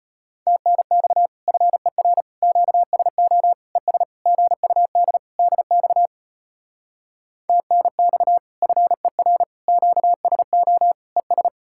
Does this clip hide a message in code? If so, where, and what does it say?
Yes: Morse code from 0.5 to 11.5 s, heard as a keyed tone, about 33 wpm, 700 Hz. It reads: TNX FER QSO ES GUD DX TNX FER QSO ES